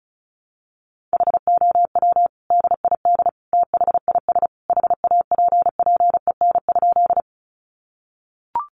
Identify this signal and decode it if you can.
Morse code, heard as a keyed tone, about 34 wpm, 700 Hz. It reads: HOW DID THIS HAPPEN?
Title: Sentences from Top 400 Words 35wpm